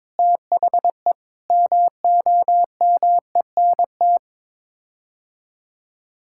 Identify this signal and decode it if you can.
Morse code, heard as a keyed tone, about 22 wpm, 700 Hz. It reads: THE MOMENT
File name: practice / 22wpm / Sets of 6 Words - 22wpm